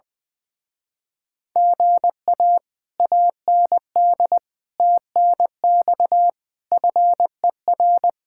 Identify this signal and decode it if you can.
Morse code, heard as a keyed tone, about 20 wpm, 700 Hz. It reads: GA AND TNX FER
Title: Speed-Racing Common QSO Phrases Encore - 20wpm